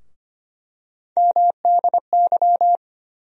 Morse code, heard as a keyed tone, about 25 wpm, 700 Hz. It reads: MDY
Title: USA Call Sign Postfixes - 3 Letters 25wpm